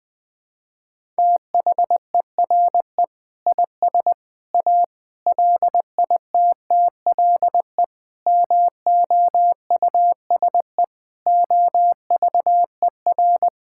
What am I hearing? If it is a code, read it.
Morse code, heard as a keyed tone, about 20 wpm, 700 Hz. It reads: THERE IS A LITTLE MOUSE OVER
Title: Sentences from Top 1000 Words - Review Part 5 - 20wpm